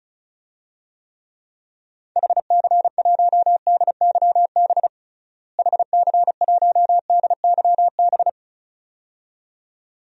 Morse code, heard as a keyed tone, about 36 wpm, 700 Hz. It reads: HC1DYB HC1DYB
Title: DXCC-100 Call Signs - 6 Characters - Repeated 2x - 35wpm